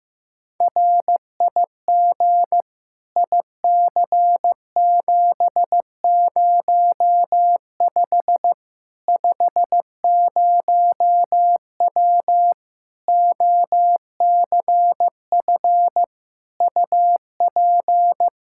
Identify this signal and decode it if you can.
Morse code, heard as a keyed tone, about 15 wpm, 700 Hz. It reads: RIG IC705 50W OCF UP
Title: Practice QSO - Exchange 2 - Info - 15wpm